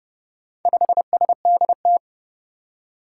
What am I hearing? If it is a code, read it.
Morse code, heard as a keyed tone, about 30 wpm, 700 Hz. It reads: 5SDT
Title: Four Challenging Letter-Number Combinations 30wpm